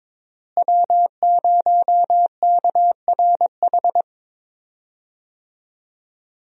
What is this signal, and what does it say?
Morse code, heard as a keyed tone, about 21 wpm, 700 Hz. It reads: W0KRH
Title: USA Call Signs - 5 Characters - Repeated 2x - 22wpm